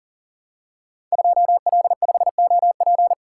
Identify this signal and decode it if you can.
Morse code, heard as a keyed tone, about 38 wpm, 700 Hz. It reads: 2L5OP
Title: Code Groups - 5 Letter:Numbers - 40wpm